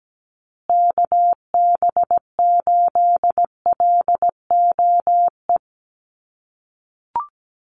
Morse code, heard as a keyed tone, about 17 wpm, 700 Hz. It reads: KB8LOE
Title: USA Call Signs Repeated 2x 17wpm